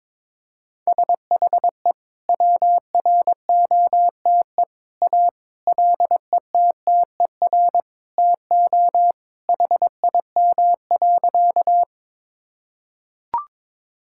Morse code, heard as a keyed tone, about 21 wpm, 700 Hz. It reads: SHE WROTE A LETTER TO HIM.
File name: practice / 22wpm / Sentences with 2-Word Phrases - 22wpm